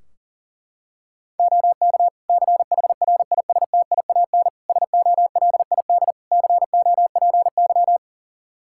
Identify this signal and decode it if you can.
Morse code, heard as a keyed tone, about 40 wpm, 700 Hz. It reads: OK CHRISTIAN SOLID COPY
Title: Practice QSO - Exchange 3 - More Info - 40wpm